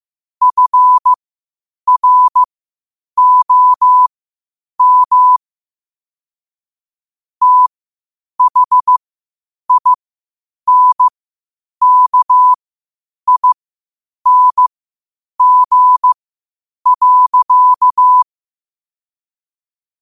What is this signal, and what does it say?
Morse code, heard as a keyed tone, about 15 wpm, spaced as for 8 wpm, 1000 Hz. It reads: FROM THINKING.
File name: qotdfiles / 8wpm